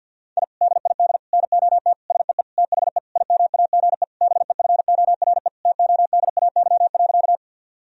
Morse code, heard as a keyed tone, about 50 wpm, 700 Hz. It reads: I DID NOT SEE THE IMAGE BEFORE TODAY.
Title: Sentences from Top 1000 Words - Review Part 4 - 50wpm